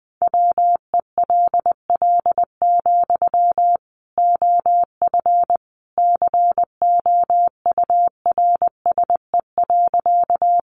Morse code, heard as a keyed tone, about 20 wpm, 700 Hz. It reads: WELL, OF COURSE.
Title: Sentences from Top 500 Words 20wpm